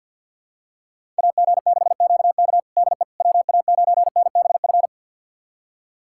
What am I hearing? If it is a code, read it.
Morse code, heard as a keyed tone, about 52 wpm, 700 Hz. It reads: AG6QK DE WA9NBF
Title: USA Callsign DE Callsign 50wpm